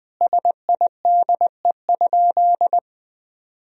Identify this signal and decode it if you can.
Morse code, heard as a keyed tone, about 20 wpm, 700 Hz. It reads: SIDE?
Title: Sentences from Top 1000 Words - Review Part 1 - 20wpm